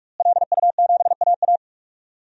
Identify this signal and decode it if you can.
Morse code, heard as a keyed tone, about 41 wpm, 700 Hz. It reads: LU7AA
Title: DXCC-100 Call Signs - 5 Characters - Repeated 3x - 45wpm